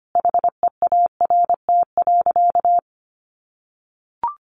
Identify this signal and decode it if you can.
Morse code, heard as a keyed tone, about 25 wpm, 700 Hz. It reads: HEART.
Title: Sentences from Top 1200 Words 25wpm